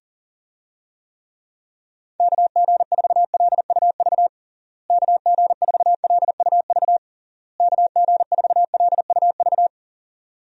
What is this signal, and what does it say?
Morse code, heard as a keyed tone, about 37 wpm, 700 Hz. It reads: KG4LUV KG4LUV KG4LUV
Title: USA Call Signs - 6 Characters - Repeated 3x - 40wpm